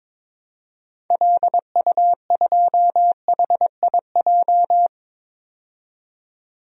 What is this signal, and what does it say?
Morse code, heard as a keyed tone, about 22 wpm, 700 Hz. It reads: LU2HIJ
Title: DXCC-100 Call Signs - 6 Characters - 22wpm